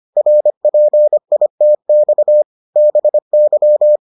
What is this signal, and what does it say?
Morse code, heard as a keyed tone, about 25 wpm, 600 Hz. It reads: RPITX BY